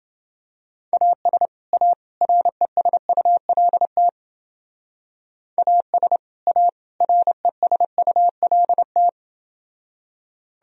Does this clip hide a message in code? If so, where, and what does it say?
Yes: Morse code from 0.9 to 9.1 s, heard as a keyed tone, about 28 wpm, 700 Hz. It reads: AS A RESULT AS A RESULT